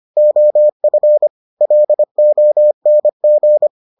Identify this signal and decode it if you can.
Morse code, heard as a keyed tone, about 25 wpm, 600 Hz. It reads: OF LONG